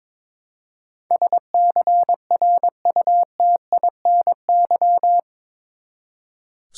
Morse code, heard as a keyed tone, about 21 wpm, 700 Hz. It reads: SCRUTINY
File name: practice / 22wpm / Eight Letter Words - 22wpm